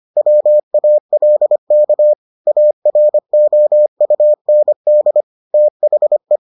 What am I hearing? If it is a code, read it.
Morse code, heard as a keyed tone, about 25 wpm, 600 Hz. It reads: WALK AROUND THE